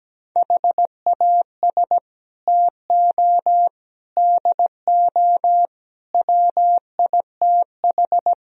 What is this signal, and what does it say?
Morse code, heard as a keyed tone, about 17 wpm, 700 Hz. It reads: HAS TO DO WITH
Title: Speed-Racing - Sets of 5 Words 17wpm